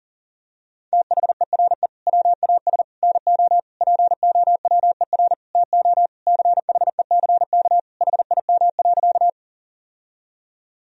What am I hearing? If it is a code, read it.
Morse code, heard as a keyed tone, about 40 wpm, 700 Hz. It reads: THERE WAS NO POWER TO CHECK HIM.
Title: Sentences from Top 1000 Words - Review Part 3 - 40wpm